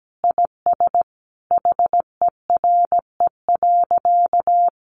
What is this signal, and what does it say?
Morse code, heard as a keyed tone, about 17 wpm, 700 Hz. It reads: IS HERE.
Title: Sentences from Top 1000 Words 17wpm